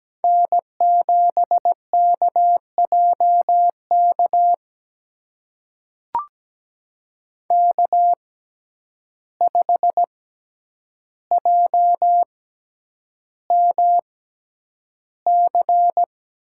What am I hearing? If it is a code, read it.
Morse code, heard as a keyed tone, about 17 wpm, spaced as for 9 wpm, 700 Hz. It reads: N7KJK K5JMC
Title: USA Call Signs - 5 Characters - 6x Character Spacing - 17wpm